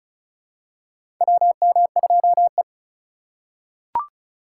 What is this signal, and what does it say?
Morse code, heard as a keyed tone, about 34 wpm, 700 Hz. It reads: WM2E
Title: USA Call Signs - 4 Characters - 3x Character Spacing - 35wpm